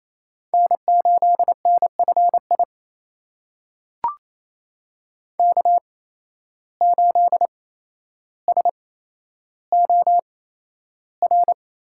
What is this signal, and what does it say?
Morse code, heard as a keyed tone, about 28 wpm, spaced as for 12 wpm, 700 Hz. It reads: N8NFI K8SOR
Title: USA Call Signs - 5 Characters - 8x Character Spacing - 28wpm